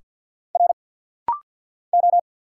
Morse code, heard as a keyed tone, about 50 wpm, 700 Hz. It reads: R O
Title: Single Letters - Rapid-Fire 50wpm